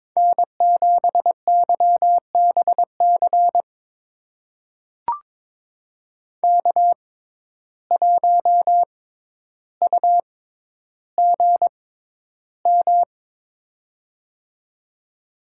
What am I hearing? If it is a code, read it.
Morse code, heard as a keyed tone, about 22 wpm, spaced as for 11 wpm, 700 Hz. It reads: N7YBC K1UGM
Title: USA Call Signs - 6x Character Spacing - 22wpm